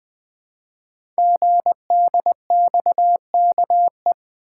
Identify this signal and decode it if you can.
Morse code, heard as a keyed tone, about 20 wpm, 700 Hz. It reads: GDXKE
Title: Code Groups - 5 Letters - 20wpm